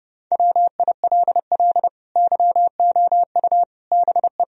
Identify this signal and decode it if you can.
Morse code, heard as a keyed tone, about 30 wpm, 700 Hz. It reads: WILL YOU BE
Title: Sentences from Top 1000 Words - Review Part 2 - 30wpm